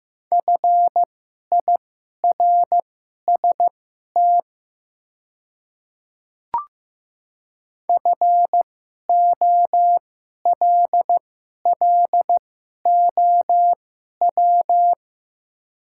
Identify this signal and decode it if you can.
Morse code, heard as a keyed tone, about 15 wpm, spaced as for 11 wpm, 700 Hz. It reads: FIRST FOLLOW
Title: 2x Character Spacing - Top 300 Words - 15wpm